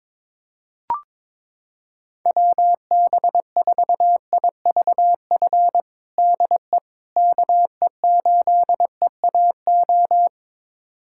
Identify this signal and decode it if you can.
Morse code, heard as a keyed tone, about 22 wpm, 700 Hz. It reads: WB4IVF DE KE8EAO
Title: USA Callsign DE Callsign 22wpm